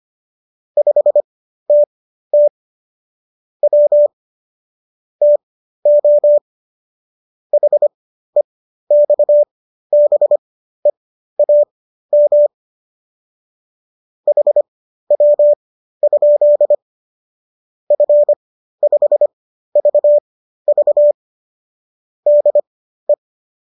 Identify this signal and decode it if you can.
Morse code, heard as a keyed tone, about 25 wpm, spaced as for 13 wpm, 590 Hz. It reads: 5TT W TO HEXBEAM HW? F5VV DE
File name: qso214 13